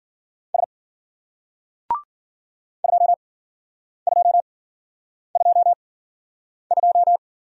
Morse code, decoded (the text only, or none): S 2222